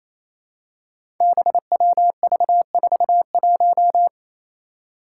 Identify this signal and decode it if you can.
Morse code, heard as a keyed tone, about 29 wpm, 700 Hz. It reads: BWV41
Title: Five Challenging Letter-Number Combinations 28wpm